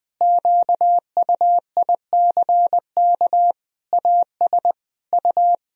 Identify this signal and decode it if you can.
Morse code, heard as a keyed tone, about 21 wpm, 700 Hz. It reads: QUICK AS U
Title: Sentences from Top 500 Words and Abbreviations 20wpm